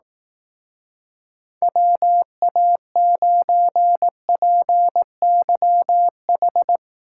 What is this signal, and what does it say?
Morse code, heard as a keyed tone, about 18 wpm, 700 Hz. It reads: WA9PYH
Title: Speed Racing - USA 2x3 Call Signs 15wpm